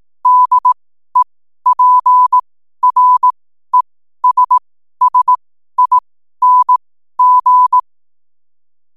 Morse code, heard as a keyed tone, about 18 wpm, spaced as for 12 wpm, 1000 Hz. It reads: DEPRESSING
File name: n7net / 12wpm